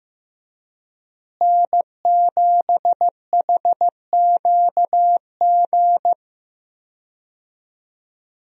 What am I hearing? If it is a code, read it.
Morse code, heard as a keyed tone, about 15 wpm, 700 Hz. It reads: N7HQG
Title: USA Call Signs - 5 Characters - Repeated 3x - 15wpm